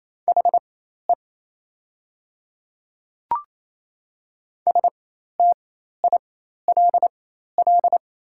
Morse code, heard as a keyed tone, about 28 wpm, spaced as for 13 wpm, 700 Hz. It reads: HE STILL
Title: 4x Character Spacing - Top 300 Words - 28wpm